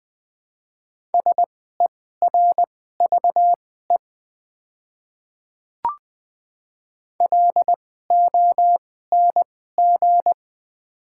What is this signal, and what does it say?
Morse code, heard as a keyed tone, about 20 wpm, spaced as for 15 wpm, 690 Hz. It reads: SERVE LONG
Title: 2x Character Spacing - Top 500 Words - 20wpm